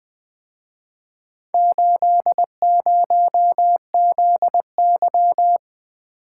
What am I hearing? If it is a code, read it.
Morse code, heard as a keyed tone, about 25 wpm, 700 Hz. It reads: 80ZY
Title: Speed-Racing - Four Letter-Number Combinations - 20wpm